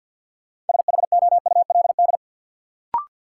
Morse code, heard as a keyed tone, about 48 wpm, 700 Hz. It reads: SHOULD
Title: Sets of 2 Words - New Word Drill - 50wpm